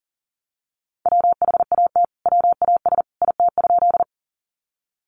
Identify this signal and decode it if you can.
Morse code, heard as a keyed tone, about 40 wpm, 700 Hz. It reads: WHAT WAS IT?
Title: Sentences from Top 100 Words 40wpm